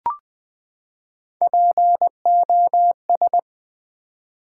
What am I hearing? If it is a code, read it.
Morse code, heard as a keyed tone, about 20 wpm, 700 Hz. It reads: POS
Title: Build A Word - 3 Letters At A Time - 8-Letter Words - 20wpm